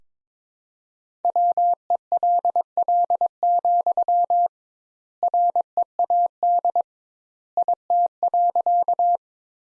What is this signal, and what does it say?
Morse code, heard as a keyed tone, about 22 wpm, 700 Hz. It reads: WELL, READ IT.
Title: Sentences Top 200 Words - 22wpm